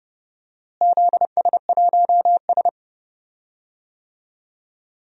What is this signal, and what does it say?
Morse code, heard as a keyed tone, about 30 wpm, 700 Hz. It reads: ZS1S
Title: DXCC-100 Call Signs - 4 Characters - Repeated 2x - 30wpm